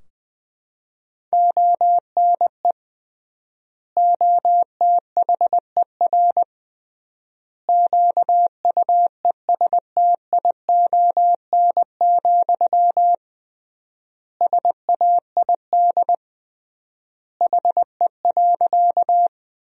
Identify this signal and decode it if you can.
Morse code, heard as a keyed tone, about 20 wpm, 700 Hz. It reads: ONE OTHER QUESTION, SAID HE.
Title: 3x Word-Spacing - Sentences from Top 400 Words - 20wpm